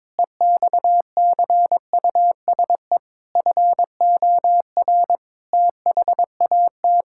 Morse code, heard as a keyed tone, about 22 wpm, 700 Hz. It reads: EXCUSE FOR THAT
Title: Sentences from Top 1000 Words - Review Part 6 - 22wpm